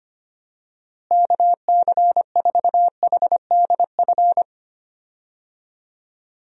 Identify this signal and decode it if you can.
Morse code, heard as a keyed tone, about 25 wpm, 700 Hz. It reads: KC4HDF
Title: USA Call Signs - 6 Characters - Repeated 3x - 25wpm